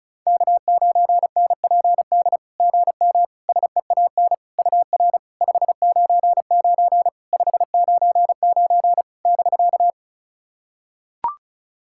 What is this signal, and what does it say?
Morse code, heard as a keyed tone, about 34 wpm, 700 Hz. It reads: K9NPD GM SEAN UR 599 599 <BK>
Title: SOTA QSO Practice 35wpm